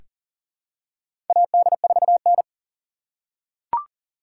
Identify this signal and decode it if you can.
Morse code, heard as a keyed tone, about 40 wpm, 700 Hz. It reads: AD4N
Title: USA Call Signs - 4 Characters - 3x Character Spacing - 40wpm